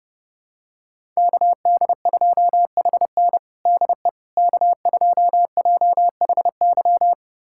Morse code, heard as a keyed tone, about 30 wpm, 700 Hz. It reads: KD2HN DE K2JHY
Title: USA Callsign DE Callsign - 5 Characters - 30wpm